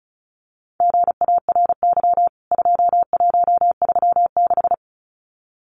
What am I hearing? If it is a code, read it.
Morse code, heard as a keyed tone, about 34 wpm, 700 Hz. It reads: GARY 2136
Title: CWT Exchange 35wpm